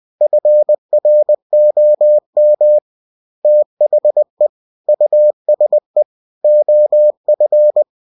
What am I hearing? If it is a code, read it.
Morse code, heard as a keyed tone, about 20 wpm, 600 Hz. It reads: FROM THE USE OF
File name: cw-books / winnie-the-pooh / winnie-20wpm-page-0117